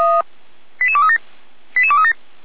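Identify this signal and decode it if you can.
Five-tone selective call (ZVEI), heard as quick runs of five tones, tones of about 70 ms. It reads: 88237 88237
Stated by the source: SDR-received ZVEI five-tone call.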